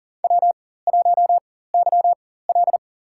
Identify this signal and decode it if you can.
Morse code, heard as a keyed tone, about 40 wpm, spaced as for 19 wpm, 700 Hz. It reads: W1YL
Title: USA Call Signs - 4 Characters - 4x Character Spacing - 40wpm